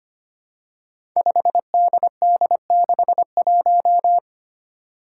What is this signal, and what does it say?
Morse code, heard as a keyed tone, about 25 wpm, 700 Hz. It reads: 5DD61
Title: Five Challenging Letter-Number Combinations 25wpm